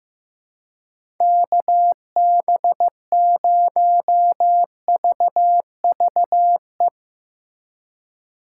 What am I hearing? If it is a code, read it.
Morse code, heard as a keyed tone, about 15 wpm, 700 Hz. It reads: KB0VVE
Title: USA Call Signs - 6 Characters - 4x Character Spacing - 15wpm